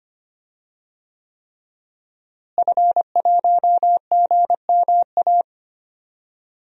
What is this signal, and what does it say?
Morse code, heard as a keyed tone, about 25 wpm, 700 Hz. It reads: F1GMA
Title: DXCC-100 Call Signs - 5 Characters - Repeated 3x - 25wpm